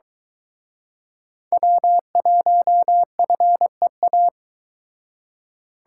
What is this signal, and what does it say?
Morse code, heard as a keyed tone, about 23 wpm, 700 Hz. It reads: W1FEA